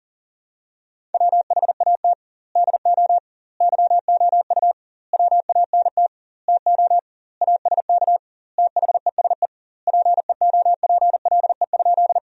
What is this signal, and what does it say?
Morse code, heard as a keyed tone, about 37 wpm, 700 Hz. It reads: WHAT DO YOU WANT TO ASK THESE PEOPLE?
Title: 2x Word-Spacing - Sentences from Top 200 Words - 40wpm